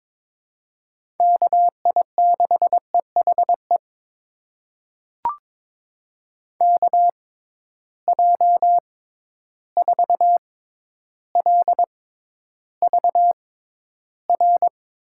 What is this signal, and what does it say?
Morse code, heard as a keyed tone, about 22 wpm, spaced as for 11 wpm, 700 Hz. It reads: KI6EHE KJ4LVR
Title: USA Call Signs - 6 Characters - 6x Character Spacing - 22wpm